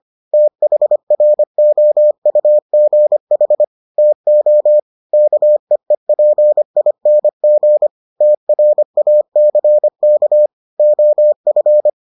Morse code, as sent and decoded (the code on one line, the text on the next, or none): - .... .-. --- ..- --. .... / - --- / -.- . .--. .. -. --. / - .-. .- -.-. -.- / --- ..-.
THROUGH TO KEEPING TRACK OF